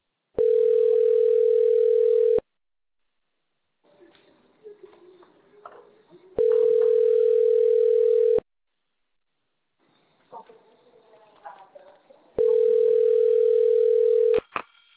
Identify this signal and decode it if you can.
Ringback tone (North America), 440 + 480 Hz.